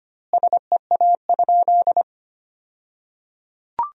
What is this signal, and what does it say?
Morse code, heard as a keyed tone, about 25 wpm, 700 Hz. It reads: SEA?
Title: 2x Word-Spacing - Sentences from Top 300 Words - 25wpm